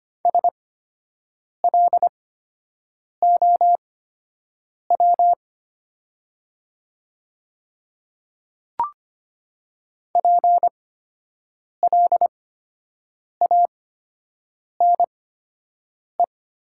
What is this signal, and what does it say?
Morse code, heard as a keyed tone, about 25 wpm, spaced as for 7 wpm, 700 Hz. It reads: SLOW PLANE
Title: 8x Character Spacing - Top 500 Words - 25wpm